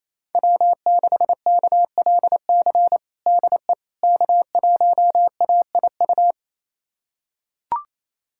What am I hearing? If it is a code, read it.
Morse code, heard as a keyed tone, about 28 wpm, 700 Hz. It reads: W6KLC DE K1AIU
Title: USA Callsign DE Callsign - 5 Characters - 28wpm